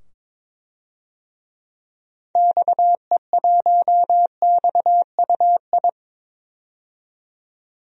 Morse code, heard as a keyed tone, about 22 wpm, 700 Hz. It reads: XE1XUI
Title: DXCC-100 Call Signs - 6 Characters - 22wpm